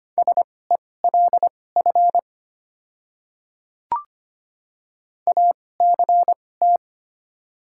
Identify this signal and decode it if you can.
Morse code, heard as a keyed tone, about 25 wpm, spaced as for 18 wpm, 700 Hz. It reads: SELF ACT
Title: 2x Character Spacing - Top 200 Words - 25wpm